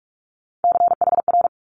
Morse code, heard as a keyed tone, about 46 wpm, 700 Hz. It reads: CHR
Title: Multi-letter Phonemes 45wpm